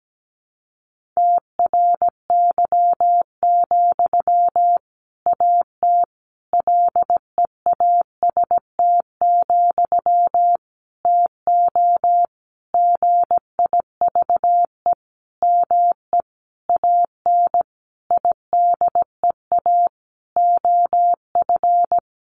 Morse code, heard as a keyed tone, about 17 wpm, 700 Hz. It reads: TRY, AT LEAST, TO GIVE ME AN IDEA OF
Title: Sentences from Top 700 Words 17wpm